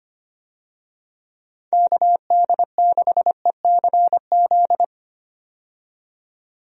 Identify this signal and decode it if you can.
Morse code, heard as a keyed tone, about 25 wpm, 700 Hz. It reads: KD6ECZ
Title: USA Call Signs - 6 Characters - 25wpm